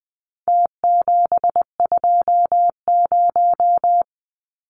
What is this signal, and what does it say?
Morse code, heard as a keyed tone, about 19 wpm, 700 Hz. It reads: T720
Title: Numbers - Contest Style - 4-Digit 20wpm